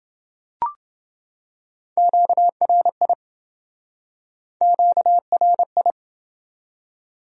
Morse code, heard as a keyed tone, about 28 wpm, 700 Hz. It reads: QRI QRI